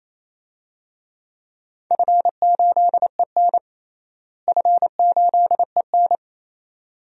Morse code, heard as a keyed tone, about 28 wpm, 700 Hz. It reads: F8EN F8EN
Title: DXCC-100 Call Signs - 4 Characters - Repeated 2x - 28wpm